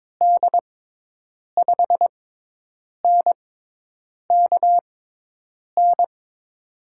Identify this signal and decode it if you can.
Morse code, heard as a keyed tone, about 22 wpm, spaced as for 8 wpm, 700 Hz. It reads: D5NKN
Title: USA Call Signs - 6 Characters - 6x Character Spacing - 22wpm